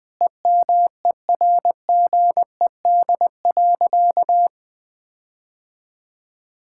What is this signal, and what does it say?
Morse code, heard as a keyed tone, about 20 wpm, 700 Hz. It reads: EMERGED.
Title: Sentences with 5-Word Phrases - Spoken Beginning - 20wpm